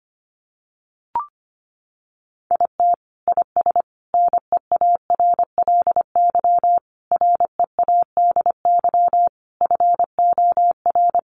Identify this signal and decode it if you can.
Morse code, heard as a keyed tone, about 25 wpm, 700 Hz. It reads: IT IS NEARLY READY FOR
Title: Sentences from Top 1500 Words 25wpm